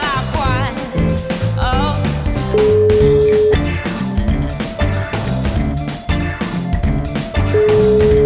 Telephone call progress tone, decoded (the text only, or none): ringback tone